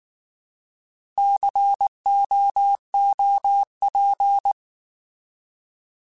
Morse code, heard as a keyed tone, about 19 wpm, 800 Hz. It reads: COOP
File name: Alphabetical / Alphabetical mp3s / Cs @18